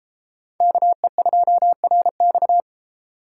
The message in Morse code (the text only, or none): KE2RX